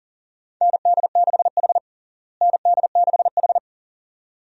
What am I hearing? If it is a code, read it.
Morse code, heard as a keyed tone, about 37 wpm, 700 Hz. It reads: ND6H ND6H
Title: USA Call Signs - 4 Characters - Repeated 2x - 40wpm